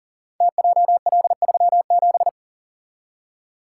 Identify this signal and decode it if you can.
Morse code, heard as a keyed tone, about 39 wpm, 700 Hz. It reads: TJL37
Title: Code Groups - 5 Letter:Numbers - 40wpm